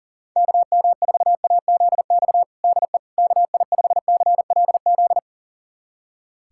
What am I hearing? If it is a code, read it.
Morse code, heard as a keyed tone, about 40 wpm, 700 Hz. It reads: KM4AZX DE KI5CLZ
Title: USA Callsign DE Callsign - 6 Characters - 40wpm